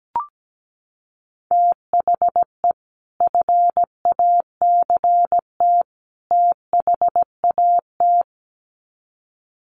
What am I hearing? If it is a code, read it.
Morse code, heard as a keyed tone, about 17 wpm, 700 Hz. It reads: THE FACT THAT
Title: Sets of 3 Words 17wpm